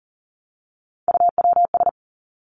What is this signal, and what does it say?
Morse code, heard as a keyed tone, about 40 wpm, 700 Hz. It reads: UWS
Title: Three Challenging Letter-Number Combinations 40wpm